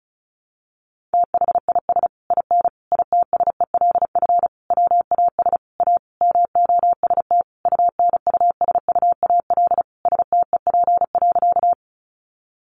Morse code, heard as a keyed tone, about 34 wpm, 700 Hz. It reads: THIS IN ITSELF WAS A MOST UNUSUAL STEP.
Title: Sentences from Top 1400 Words 35wpm